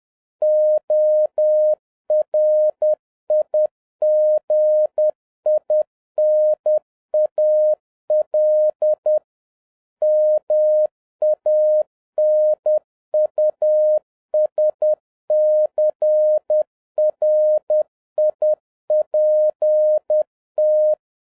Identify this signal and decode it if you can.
Morse code, heard as a keyed tone, about 10 wpm, 610 Hz. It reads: ORIGINAL MANUSCRIPT
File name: 2011-07-mp3 / n21 GREETINGS FROM SINGAPORE 10wpm